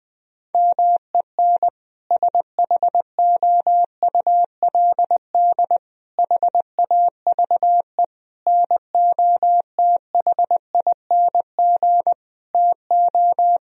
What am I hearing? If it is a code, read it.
Morse code, heard as a keyed tone, about 20 wpm, 700 Hz. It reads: MEN SHOULD HAVE NOTHING TO
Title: Sentences from Top 1000 Words - Review Part 3 - 20wpm